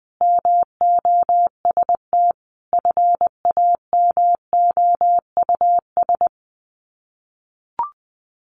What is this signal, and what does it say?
Morse code, heard as a keyed tone, about 20 wpm, 700 Hz. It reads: MOST FAMOUS
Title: Sets of 5 Words Encore2 20wpm